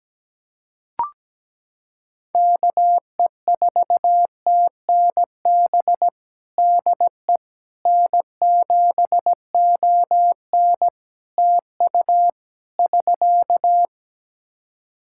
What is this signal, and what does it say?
Morse code, heard as a keyed tone, about 17 wpm, 700 Hz. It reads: KE4TNB DE N7ON TU <SK>
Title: Practice QSO - Exchange 4 - Ending - 17wpm